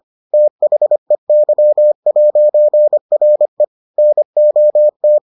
Morse code, heard as a keyed tone, about 25 wpm, 600 Hz. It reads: THEY'RE NOT